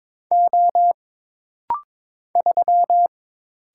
Morse code, heard as a keyed tone, about 22 wpm, 700 Hz. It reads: O 3
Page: This rapid-fire timing is geared to push your ICR (Instant Character Recognition) skills.